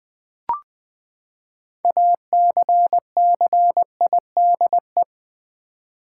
Morse code, heard as a keyed tone, about 20 wpm, 700 Hz. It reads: ACCIDE
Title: Build A Word - 6 Letters At A Time - 12-Letter Words - 20wpm